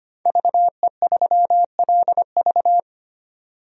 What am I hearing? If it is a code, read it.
Morse code, heard as a keyed tone, about 25 wpm, 700 Hz. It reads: VE3LV
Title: DXCC-100 Call Signs - 5 Characters - 25wpm